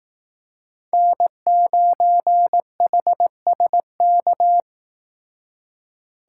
Morse code, heard as a keyed tone, about 18 wpm, 700 Hz. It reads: N9HSK
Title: Speed Racing - USA 1x3 Call Signs 15wpm